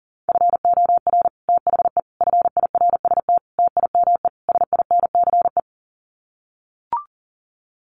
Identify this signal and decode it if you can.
Morse code, heard as a keyed tone, about 40 wpm, 700 Hz. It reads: FOR THE FIRST TIME SINCE
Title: Sets of 5 Words 40wpm